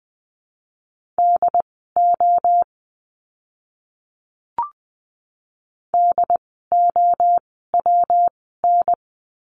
Morse code, heard as a keyed tone, about 20 wpm, spaced as for 14 wpm, 700 Hz. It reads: DO DOWN
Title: 237 Lesson 38f - Letter q - Words 20wpm